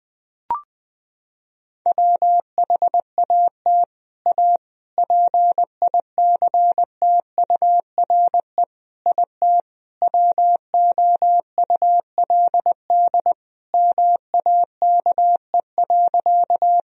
Morse code, heard as a keyed tone, about 21 wpm, 700 Hz. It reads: WHAT A PICTURE IT WOULD MAKE.
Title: Sentences from Top 1000 Words - Review Part 1 - 20wpm